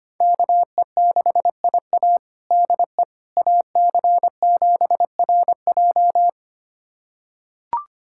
Morse code, heard as a keyed tone, about 25 wpm, 700 Hz. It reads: KE6IA DE AC7RJ
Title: USA Callsign DE Callsign - 5 Characters - 25wpm